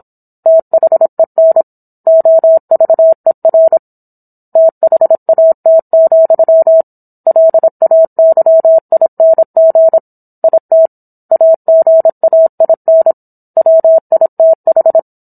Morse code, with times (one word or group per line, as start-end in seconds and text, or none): 0.5-1.6 THEN
2.1-3.8 OVER
4.6-6.8 THAT,
7.3-10.0 LAYING
10.4-10.9 IT
11.3-13.1 AGAIN
13.6-15.0 WITH